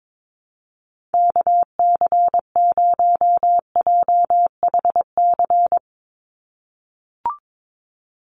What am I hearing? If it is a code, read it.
Morse code, heard as a keyed tone, about 22 wpm, 700 Hz. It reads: KC0JHC
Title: USA Call Signs 22wpm